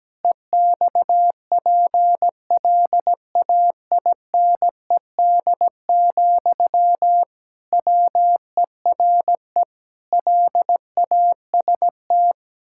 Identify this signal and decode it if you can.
Morse code, heard as a keyed tone, about 17 wpm, 700 Hz. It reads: EXPLAINED, WERE LAST
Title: Pangram Challenge v2 - 17wpm